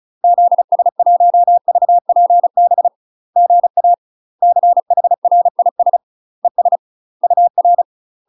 Morse code, heard as a keyed tone, about 34 wpm, 700 Hz. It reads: ZS1VPB GA CHRIS ES UR
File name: UK MASTER QSOs TRAINING / 35WPM / UK MASTER QSOs 0021